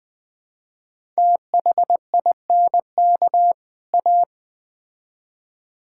Morse code, heard as a keyed tone, about 20 wpm, 700 Hz. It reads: THINK A
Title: Build A Phrase - 5-Word Phrases - 20wpm